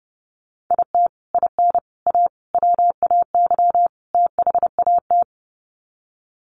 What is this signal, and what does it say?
Morse code, heard as a keyed tone, about 30 wpm, 700 Hz. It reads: IT IN A WAY THAT
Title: Sets of 5 Words Encore2 30wpm